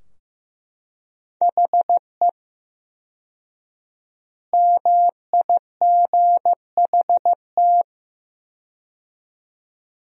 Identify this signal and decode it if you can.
Morse code, heard as a keyed tone, about 15 wpm, 700 Hz. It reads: HE MIGHT
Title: 4x Word-Spacing - Sentences from Top 300 Words - 15wpm